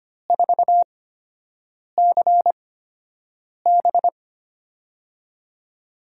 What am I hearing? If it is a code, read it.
Morse code, heard as a keyed tone, about 25 wpm, spaced as for 7 wpm, 700 Hz. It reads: 4CB
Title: USA Call Signs - 8x Character Spacing - 25wpm